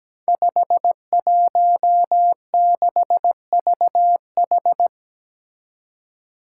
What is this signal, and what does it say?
Morse code, heard as a keyed tone, about 17 wpm, 700 Hz. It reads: 516VH
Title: Five Challenging Letter-Number Combinations 17wpm